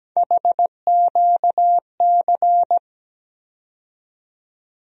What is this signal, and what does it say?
Morse code, heard as a keyed tone, about 17 wpm, 700 Hz. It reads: HQC
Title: USA Call Sign Postfixes - 3 Letters 17wpm